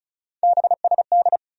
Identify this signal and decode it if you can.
Morse code, heard as a keyed tone, about 34 wpm, 700 Hz. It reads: BSD
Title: USA Call Sign Postfixes - 3 Letters 35wpm